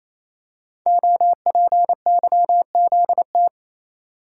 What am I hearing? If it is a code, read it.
Morse code, heard as a keyed tone, about 27 wpm, 700 Hz. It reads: OPYZT
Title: Code Groups - 5 Letters - 28wpm